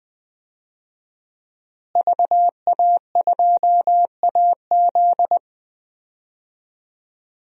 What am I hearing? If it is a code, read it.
Morse code, heard as a keyed tone, about 20 wpm, 700 Hz. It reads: VA2AZ
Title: DXCC-100 Call Signs - 5 Characters - Repeated 3x - 20wpm